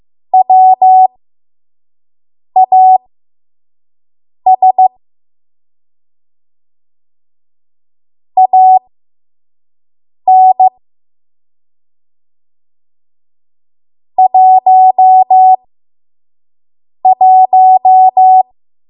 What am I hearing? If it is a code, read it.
Morse code, heard as a keyed tone, about 15 wpm, spaced as for 5 wpm, 750 Hz. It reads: WAS AN 11
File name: Morse / 05WPM